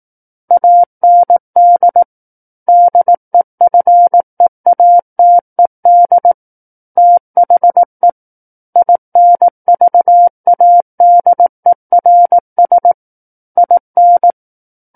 Morse code, heard as a keyed tone, about 18 wpm, 700 Hz. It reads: AND DEFEATED THE INVADERS IN
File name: 18 WPM